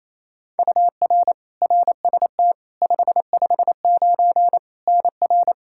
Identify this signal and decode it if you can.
Morse code, heard as a keyed tone, about 28 wpm, 700 Hz. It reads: UR RST 559 NR
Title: Practice QSO - Exchange 1 - Beginning - 28wpm